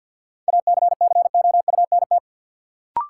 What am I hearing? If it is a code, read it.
Morse code, heard as a keyed tone, about 51 wpm, 700 Hz. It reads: ACCOUNT
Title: Sentences from Top 900 Words - New Word Drill - 50wpm